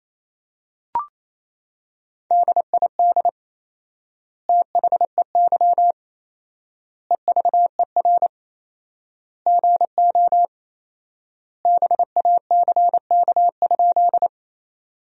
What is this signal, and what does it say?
Morse code, heard as a keyed tone, about 28 wpm, 700 Hz. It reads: DID THEY EVER GO BACK?
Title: 4x Word-Spacing - Sentences from Top 400 Words - 28wpm